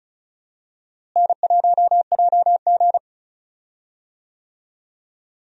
Morse code, heard as a keyed tone, about 36 wpm, 700 Hz. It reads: N1JG
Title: USA Call Signs - 4 Characters - Repeated 3x - 35wpm